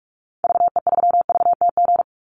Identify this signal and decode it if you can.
Morse code, heard as a keyed tone, about 44 wpm, 700 Hz. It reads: VE3VTG
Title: Call Signs - 45wpm